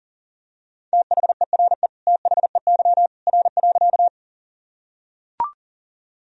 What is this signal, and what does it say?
Morse code, heard as a keyed tone, about 40 wpm, 700 Hz. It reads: THERE THEY R.
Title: Sentences from Top 100 Words and Abbreviations 40wpm